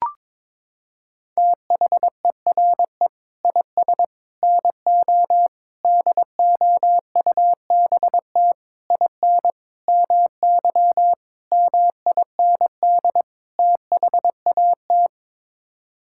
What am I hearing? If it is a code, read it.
Morse code, heard as a keyed tone, about 22 wpm, 700 Hz. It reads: THERE IS NO DOUBT IN MY MIND THAT
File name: practice / 22wpm / Sets of 8 Words - 22wpm